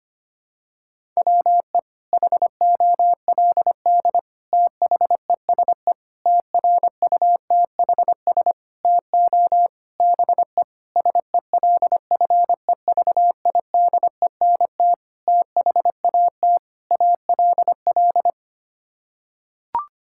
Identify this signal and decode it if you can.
Morse code, heard as a keyed tone, about 25 wpm, 700 Hz. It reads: WE HOLD THESE TRUTHS TO BE SELFEVIDENT THAT ALL
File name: Sets of 9 Words - 25wpm